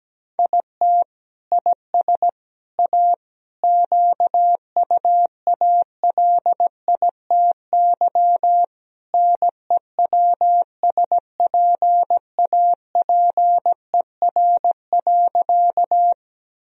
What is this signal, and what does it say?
Morse code, heard as a keyed tone, about 17 wpm, 700 Hz. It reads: IT IS A QUALITY NEWSPAPER.
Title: Sentences from Top 1000 Words - Review Part 4 - 17wpm